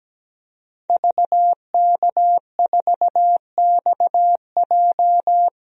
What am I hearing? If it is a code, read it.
Morse code, heard as a keyed tone, about 17 wpm, 700 Hz. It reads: VK4XJ
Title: DXCC-100 Call Signs - 5 Characters - 17wpm